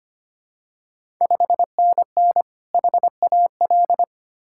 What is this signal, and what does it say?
Morse code, heard as a keyed tone, about 25 wpm, 700 Hz. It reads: 5NN HAL
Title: Ontario QSO Party Exchange 25wpm